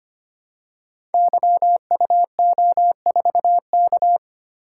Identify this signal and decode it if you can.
Morse code, heard as a keyed tone, about 25 wpm, 700 Hz. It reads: YUO4K
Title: Code Groups - 5 Letter:Numbers - 25wpm